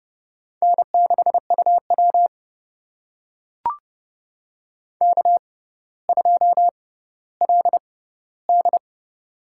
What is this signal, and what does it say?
Morse code, heard as a keyed tone, about 30 wpm, spaced as for 15 wpm, 700 Hz. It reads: N6UW K2LD
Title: USA Call Signs - 4 Characters - 6x Character Spacing - 30wpm